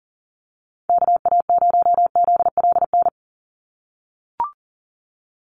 Morse code, heard as a keyed tone, about 40 wpm, 700 Hz. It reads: KA0ZLN
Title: USA Call Signs 40wpm